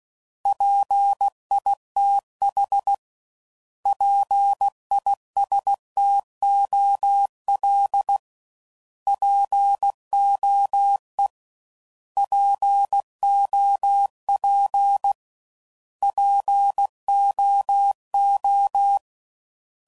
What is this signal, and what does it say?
Morse code, heard as a keyed tone, about 16 wpm, 800 Hz. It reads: PITH PISTOL POE POP POO
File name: Prefixes / By First Letter / Ps